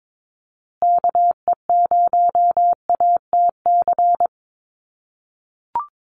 Morse code, heard as a keyed tone, about 22 wpm, 700 Hz. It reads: KE0ATC
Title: USA Call Signs Repeated 3x 22wpm